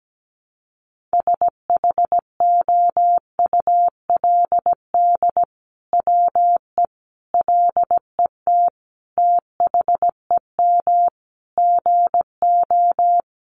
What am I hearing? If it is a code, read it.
Morse code, heard as a keyed tone, about 17 wpm, 700 Hz. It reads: SHOULD WE LET THEM GO
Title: Sentences from Top 600 Words 17wpm